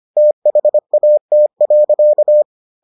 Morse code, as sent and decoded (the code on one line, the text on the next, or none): - .... .- - .-.-.-
THAT.